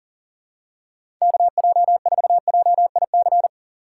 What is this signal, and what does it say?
Morse code, heard as a keyed tone, about 40 wpm, 700 Hz. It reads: KJ4JIC
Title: USA Call Signs - 6 Characters - 2x Character Spacing - 40wpm